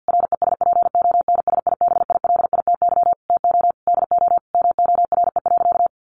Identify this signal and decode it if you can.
Morse code, heard as a keyed tone, about 50 wpm, 700 Hz. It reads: RESPONSIBILITY TO DO MORE.
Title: Sentences from Top 1400 Words 50wpm